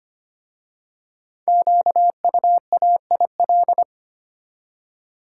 Morse code, heard as a keyed tone, about 25 wpm, 700 Hz. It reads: QUAIL
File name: practice / 25wpm / Difficult Words with Hints - 25wpm